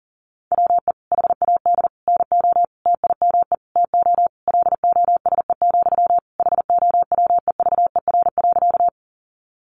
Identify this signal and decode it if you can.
Morse code, heard as a keyed tone, about 40 wpm, 700 Hz. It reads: WE HAD NO TIME TO LOSE, HOWEVER.
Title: Sentences from Top 1100 Words 40wpm